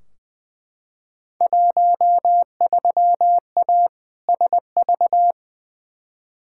Morse code, heard as a keyed tone, about 20 wpm, 700 Hz. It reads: 13A SV
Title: Field Day Exchanges Encore - 20wpm